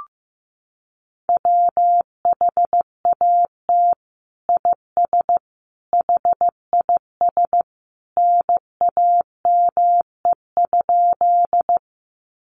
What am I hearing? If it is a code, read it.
Morse code, heard as a keyed tone, about 15 wpm, 700 Hz. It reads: WHAT IS HIS NAME?
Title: Sentences from Top 200 Words 15wpm